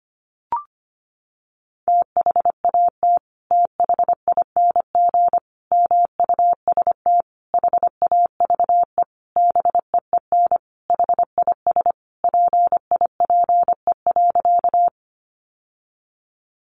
Morse code, heard as a keyed tone, about 25 wpm, 700 Hz. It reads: THAT THING MUST HAVE BEEN HIS PIPE.
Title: Sentences from Top 1500 Words 25wpm